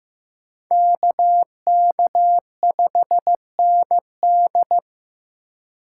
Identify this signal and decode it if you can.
Morse code, heard as a keyed tone, about 15 wpm, 700 Hz. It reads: KK5ND
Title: USA Call Signs - 3x Character Spacing - 15wpm